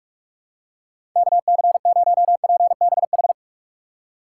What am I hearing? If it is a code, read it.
Morse code, heard as a keyed tone, about 46 wpm, 700 Hz. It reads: KC0PBH
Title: USA Call Signs - 2x Character Spacing - 45wpm